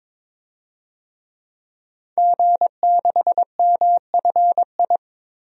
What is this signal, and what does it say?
Morse code, heard as a keyed tone, about 22 wpm, 700 Hz. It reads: G6MFI
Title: DXCC-100 Call Signs - 5 Characters - 22wpm